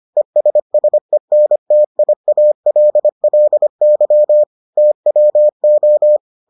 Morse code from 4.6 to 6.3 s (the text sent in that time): TWO